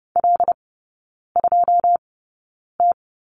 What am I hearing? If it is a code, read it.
Morse code, heard as a keyed tone, about 30 wpm, spaced as for 9 wpm, 700 Hz. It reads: L2T
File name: Lesson 18e - Letter m - Three Characters 30wpm